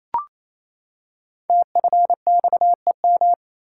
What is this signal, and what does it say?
Morse code, heard as a keyed tone, about 29 wpm, 700 Hz. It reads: TFXEM
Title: Code Groups - 5 Letters - 28wpm